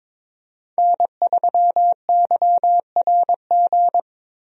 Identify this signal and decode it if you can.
Morse code, heard as a keyed tone, about 22 wpm, 700 Hz. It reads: N3YRG
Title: USA Call Signs - 5 Characters - Repeated 3x - 22wpm